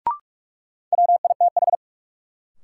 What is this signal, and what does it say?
Morse code, heard as a keyed tone, about 46 wpm, 700 Hz. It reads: WITH
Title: Sets of 2 Words - New Word Drill - 45wpm